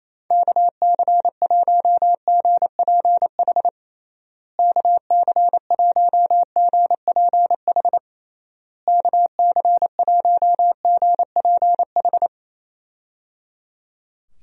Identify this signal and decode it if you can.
Morse code, heard as a keyed tone, about 28 wpm, 700 Hz. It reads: KC1GPH KC1GPH KC1GPH
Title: USA Call Signs - 6 Characters - Repeated 3x - 28wpm